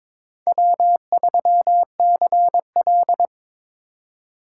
Morse code, heard as a keyed tone, about 22 wpm, 700 Hz. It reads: W3CL
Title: USA Call Signs - 4 Characters - 2x Character Spacing - 22wpm